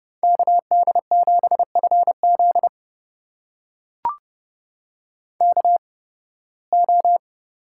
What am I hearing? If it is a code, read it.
Morse code, heard as a keyed tone, about 30 wpm, spaced as for 20 wpm, 700 Hz. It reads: KD7FZ KO
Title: USA Call Signs - 8x Character Spacing - 30wpm